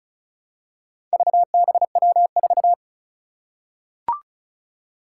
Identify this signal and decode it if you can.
Morse code, heard as a keyed tone, about 36 wpm, 700 Hz. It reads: VBW4
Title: Four Challenging Letter-Number Combinations 35wpm